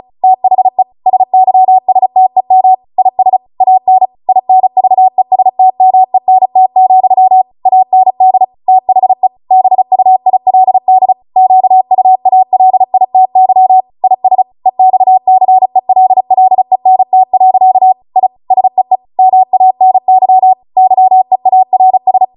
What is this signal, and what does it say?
Morse code, heard as a keyed tone, about 36 wpm, 750 Hz. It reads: THE SYSTEM IS AN INVESTMENT, AND THE BUILD QUALITY IS EXCELLENT. I SEE MANY YEARS